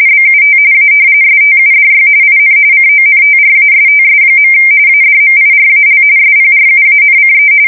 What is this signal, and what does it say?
RTTY, about 45.45 baud. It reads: BUT A STUDY IN CONTRASTS. ADMIRAL SPRUANCE
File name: fsk170 / battleshipnj